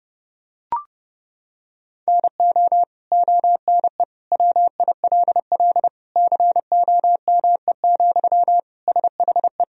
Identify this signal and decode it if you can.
Morse code, heard as a keyed tone, about 30 wpm, 700 Hz. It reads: NO ONE WILL COME, SHE
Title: Sentences from Top 100 Words and Abbreviations 30wpm